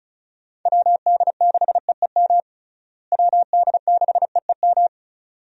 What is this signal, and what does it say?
Morse code, heard as a keyed tone, about 36 wpm, 700 Hz. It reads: WD6EEM WD6EEM
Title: USA Call Signs - 6 Characters - Repeated 2x - 35wpm